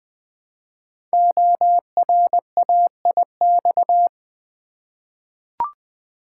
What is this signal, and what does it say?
Morse code, heard as a keyed tone, about 20 wpm, 700 Hz. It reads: ORAIX
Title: Code Groups - 5 Letters - 20wpm